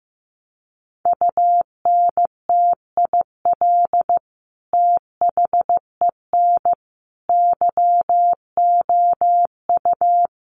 Morse code, heard as a keyed tone, about 15 wpm, 700 Hz. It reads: UNTIL THEN YOU
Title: Sentences from Top 600 Words 15wpm